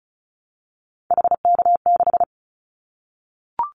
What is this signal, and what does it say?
Morse code, heard as a keyed tone, about 34 wpm, 700 Hz. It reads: HK6
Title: Callsign Prefixes for Top 100 DXCC - 35wpm